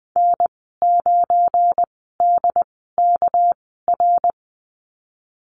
Morse code, heard as a keyed tone, about 20 wpm, spaced as for 15 wpm, 700 Hz. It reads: N9DKR
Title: 200 Lesson 32g - Letter k - Call Signs 20wpm